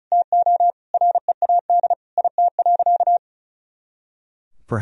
Morse code, heard as a keyed tone, about 36 wpm, 700 Hz. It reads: TO READ IT.
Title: Sentences from Top 500 Words and Abbreviations 35wpm